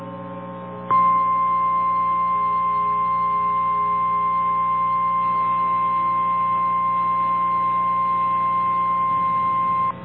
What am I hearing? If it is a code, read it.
Emergency Alert System (SAME) tones (the 1050 Hz attention tone). It is heard from 1.1 to 9.4 s.
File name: eas / KNCYKBIE / KNCYKBIE